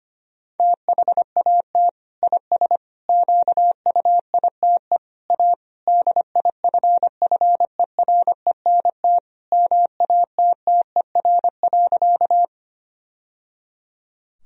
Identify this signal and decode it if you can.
Morse code, heard as a keyed tone, about 25 wpm, 700 Hz. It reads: THAT IS QUITE A DIFFERENT MATTER.
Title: Sentences from Top 1000 Words - Review Part 3 - 25wpm